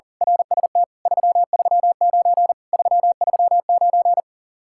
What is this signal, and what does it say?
Morse code, heard as a keyed tone, about 38 wpm, 700 Hz. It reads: RST 339 339
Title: Speed-Racing Common QSO Phrases Encore - 40wpm